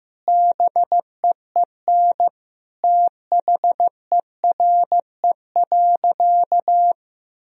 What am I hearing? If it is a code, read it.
Morse code, heard as a keyed tone, about 15 wpm, 700 Hz. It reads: BEEN THERE.
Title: Sentences from Top 1000 Words - Review Part 3 - 15wpm